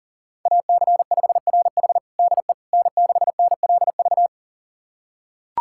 Morse code, heard as a keyed tone, about 40 wpm, 700 Hz. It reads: AC5RH DE N6NLV
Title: USA Callsign DE Callsign 40wpm